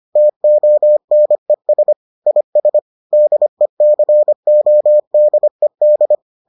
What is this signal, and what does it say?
Morse code, heard as a keyed tone, about 25 wpm, 600 Hz. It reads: TONES IS DECODED